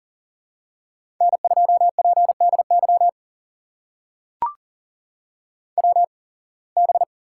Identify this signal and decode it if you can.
Morse code, heard as a keyed tone, about 39 wpm, spaced as for 26 wpm, 700 Hz. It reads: N2PDY WB
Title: USA Call Signs - 5 Characters - 8x Character Spacing - 40wpm